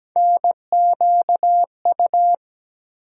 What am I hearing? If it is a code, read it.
Morse code, heard as a keyed tone, about 17 wpm, 700 Hz. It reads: NQU
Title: USA Call Sign Postfixes - 3 Letters 17wpm